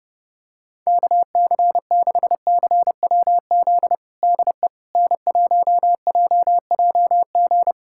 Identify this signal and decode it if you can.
Morse code, heard as a keyed tone, about 30 wpm, 700 Hz. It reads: KC6CWZ DE N1JJG
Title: USA Callsign DE Callsign 30wpm